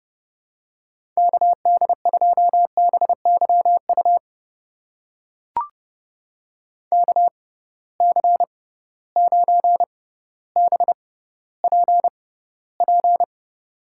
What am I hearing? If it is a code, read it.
Morse code, heard as a keyed tone, about 30 wpm, spaced as for 15 wpm, 700 Hz. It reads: KD2BYU KC9BPP
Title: USA Call Signs - 6 Characters - 6x Character Spacing - 30wpm